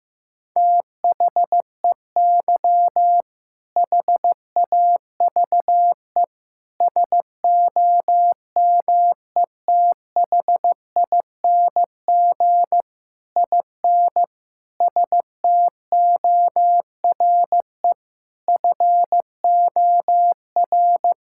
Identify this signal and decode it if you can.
Morse code, heard as a keyed tone, about 15 wpm, 700 Hz. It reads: THEY HAVE SOMETHING IN STORE FOR
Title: Sentences from Top 1000 Words - Review Part 5 - 15wpm